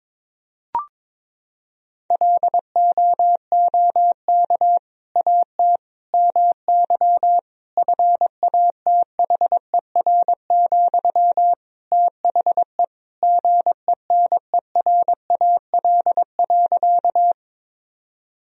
Morse code, heard as a keyed tone, about 22 wpm, 700 Hz. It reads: LOOK AT MY FATHER, THE GENERAL.
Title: Sentences from Top 1000 Words - Review Part 3 - 22wpm